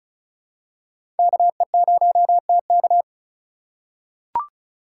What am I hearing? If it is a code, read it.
Morse code, heard as a keyed tone, about 34 wpm, 700 Hz. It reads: KE0TK
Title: USA Call Signs - 4x Character Spacing - 35wpm